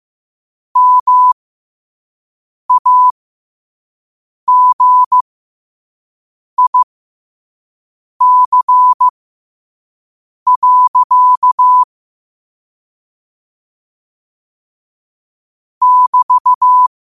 Morse code, heard as a keyed tone, about 15 wpm, spaced as for 5 wpm, 1000 Hz. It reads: MAGIC. <BT>
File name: qotdfiles / 5wpm